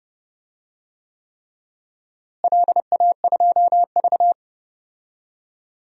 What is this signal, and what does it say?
Morse code, heard as a keyed tone, about 29 wpm, 700 Hz. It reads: LA2V
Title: DXCC-100 Call Signs - 4 Characters - Repeated 3x - 30wpm